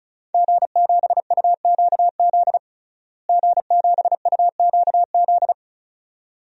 Morse code, heard as a keyed tone, about 34 wpm, 700 Hz. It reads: G7UQZ G7UQZ
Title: DXCC-100 Call Signs - 5 Characters - Repeated 2x - 35wpm